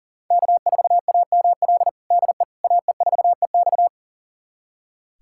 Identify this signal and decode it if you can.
Morse code, heard as a keyed tone, about 40 wpm, 700 Hz. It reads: K4AML DE AE4EX
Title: USA Callsign DE Callsign - 5 Characters - 40wpm